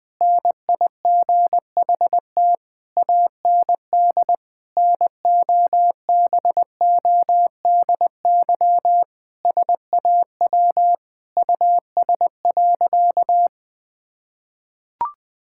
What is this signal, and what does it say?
Morse code, heard as a keyed tone, about 20 wpm, 700 Hz. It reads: NIGHT AND NOBODY SAW US.
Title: Sentences from Top 1600 Words 20wpm